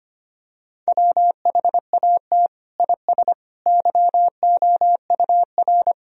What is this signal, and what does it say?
Morse code, heard as a keyed tone, about 25 wpm, 700 Hz. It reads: WHAT IS YOUR
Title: Sentences from Top 1000 Words - Review Part 4 - 25wpm